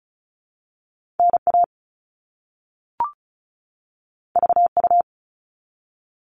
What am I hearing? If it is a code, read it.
Morse code, heard as a keyed tone, about 36 wpm, 700 Hz. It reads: NA VU
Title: All Cut Numbers - 2-Digit 35wpm